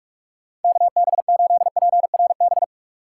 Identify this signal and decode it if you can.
Morse code, heard as a keyed tone, about 44 wpm, 700 Hz. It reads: KB8PRB